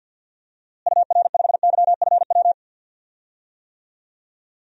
Morse code, heard as a keyed tone, about 52 wpm, 700 Hz. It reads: UR5YFW
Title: DXCC-100 Call Signs - 6 Characters - Repeated 2x - 50wpm